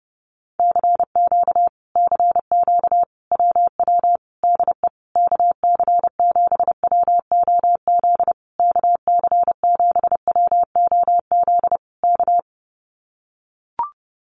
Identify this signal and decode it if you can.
Morse code, heard as a keyed tone, about 30 wpm, 700 Hz. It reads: CQ CQ WW DE KC7WOZ KC7WOZ K
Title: Common QSO Phrases Encore 30wpm